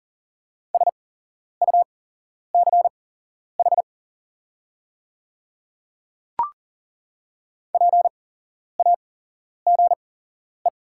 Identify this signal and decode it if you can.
Morse code, heard as a keyed tone, about 40 wpm, spaced as for 11 wpm, 700 Hz. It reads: SUCH PAGE
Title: 8x Character Spacing - Top 500 Words - 40wpm